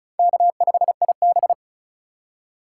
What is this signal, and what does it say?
Morse code, heard as a keyed tone, about 34 wpm, 700 Hz. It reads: K5IB
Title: USA Call Signs - 4 Characters - 2x Character Spacing - 35wpm